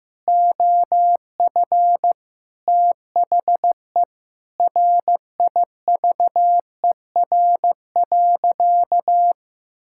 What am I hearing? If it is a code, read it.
Morse code, heard as a keyed tone, about 15 wpm, 700 Hz. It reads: OF THE RIVER.
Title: Sentences from Top 1000 Words - Review Part 6 - 15wpm